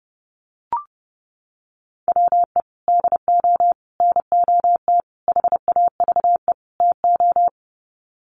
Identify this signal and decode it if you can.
Morse code, heard as a keyed tone, about 30 wpm, 700 Hz. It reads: WE DO NOT HAVE TO
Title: Sets of 5 Words 30wpm